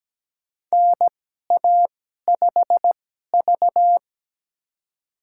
Morse code, heard as a keyed tone, about 17 wpm, spaced as for 12 wpm, 700 Hz. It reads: NA5V
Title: USA Call Signs - 2x Character Spacing - 17wpm